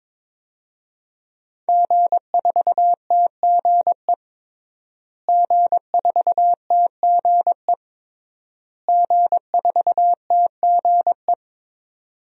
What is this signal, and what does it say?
Morse code, heard as a keyed tone, about 22 wpm, 700 Hz. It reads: G4TGE G4TGE G4TGE
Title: DXCC-100 Call Signs - 5 Characters - Repeated 3x - 22wpm